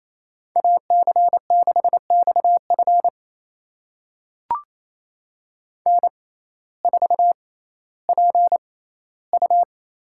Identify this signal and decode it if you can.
Morse code, heard as a keyed tone, about 28 wpm, spaced as for 15 wpm, 700 Hz. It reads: AC6XF N4PU
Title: USA Call Signs - 5 Characters - 6x Character Spacing - 28wpm